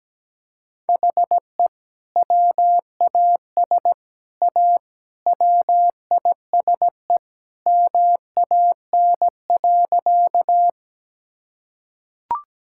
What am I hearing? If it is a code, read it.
Morse code, heard as a keyed tone, about 17 wpm, 700 Hz. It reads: HE WAS A WISE MAN.
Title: Sentences from Top 1600 Words 17wpm